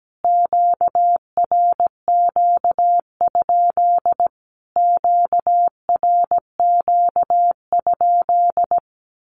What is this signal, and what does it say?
Morse code, heard as a keyed tone, about 17 wpm, 700 Hz. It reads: QRQ? QRQ?
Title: Q-Codes Repeated 3x 17wpm